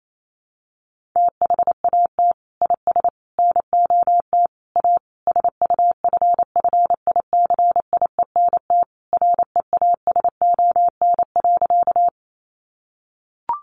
Morse code, heard as a keyed tone, about 28 wpm, 700 Hz. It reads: THAT IS NOT A SUFFICIENT REASON.
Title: Sentences from Top 1400 Words 28wpm